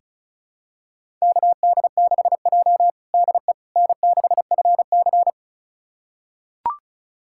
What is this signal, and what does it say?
Morse code, heard as a keyed tone, about 36 wpm, 700 Hz. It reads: KD6J DE N6FC
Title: USA Callsign DE Callsign - 4 Characters - 35wpm